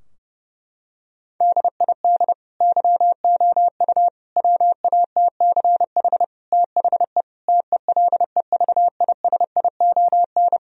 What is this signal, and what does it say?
Morse code, heard as a keyed tone, about 30 wpm, 700 Hz. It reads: DID YOU WATCH THE TELEVISION
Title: Sentences from Top 1000 Words - Review Part 4 - 30wpm